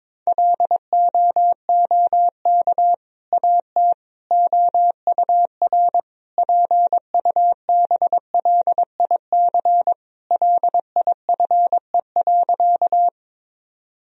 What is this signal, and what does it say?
Morse code, heard as a keyed tone, about 21 wpm, 700 Hz. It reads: LOOK AT OUR PUBLIC LIFE.
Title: Sentences from Top 1000 Words - Review Part 3 - 22wpm